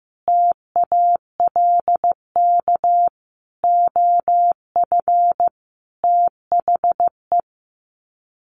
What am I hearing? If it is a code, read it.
Morse code, heard as a keyed tone, about 15 wpm, 700 Hz. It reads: TALK OF THE
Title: Sets of 5 Words 15wpm